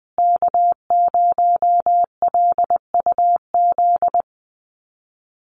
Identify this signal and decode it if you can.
Morse code, heard as a keyed tone, about 20 wpm, 700 Hz. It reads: K0LUZ
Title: Call Signs Repeated 2x - 20wpm